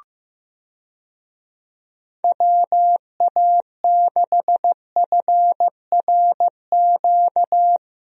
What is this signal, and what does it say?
Morse code, heard as a keyed tone, about 15 wpm, 700 Hz. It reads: WA6FRQ
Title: USA Call Signs - 6 Characters - Repeated 2x - 15wpm